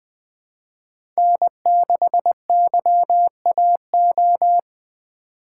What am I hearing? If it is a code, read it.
Morse code, heard as a keyed tone, about 20 wpm, 700 Hz. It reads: N6YAO
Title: USA Call Signs - 5 Characters - 3x Character Spacing - 20wpm